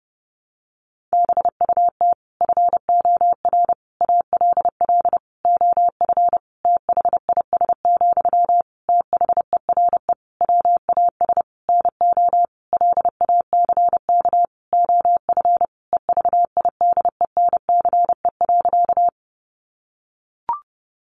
Morse code, heard as a keyed tone, about 30 wpm, 700 Hz. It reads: BUT FOR ALL OF THIS, THERE WAS NO LACK OF EVIDENCE.
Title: Sentences from Top 1300 Words 30wpm